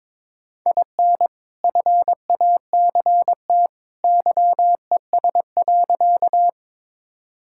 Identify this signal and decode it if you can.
Morse code, heard as a keyed tone, about 22 wpm, 700 Hz. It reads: IN FACT YES.
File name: practice / 22wpm / Sentences with 2-Word Phrases - 22wpm